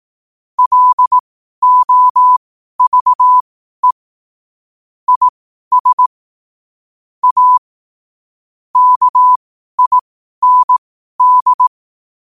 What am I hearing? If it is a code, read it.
Morse code, heard as a keyed tone, about 18 wpm, spaced as for 12 wpm, 1000 Hz. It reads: LOVE IS A KIND
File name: qotdfiles / 12wpm